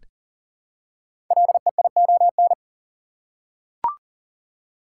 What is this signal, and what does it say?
Morse code, heard as a keyed tone, about 40 wpm, 700 Hz. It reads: LEION
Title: Code Groups - 5 Letter:Numbers - 40wpm